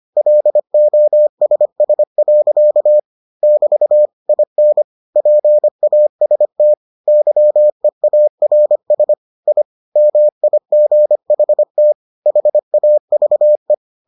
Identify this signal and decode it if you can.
Morse code, heard as a keyed tone, about 25 wpm, 600 Hz. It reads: LOSS. <BT> IN PAST YEARS I MIGHT HAVE